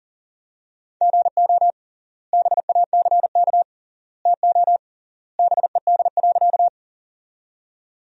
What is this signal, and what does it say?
Morse code, heard as a keyed tone, about 40 wpm, 700 Hz. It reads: GO BACK TO BED.
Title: 3x Word-Spacing - Sentences from Top 500 Words - 40wpm